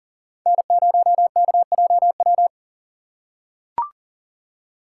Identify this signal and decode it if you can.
Morse code, heard as a keyed tone, about 40 wpm, 700 Hz. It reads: N0KJW
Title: USA Call Signs - 5 Characters - 3x Character Spacing - 40wpm